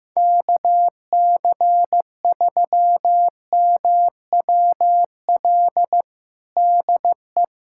Morse code, heard as a keyed tone, about 15 wpm, 700 Hz. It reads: KC3MWL DE
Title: USA Callsign DE Callsign - 6 Characters - 15wpm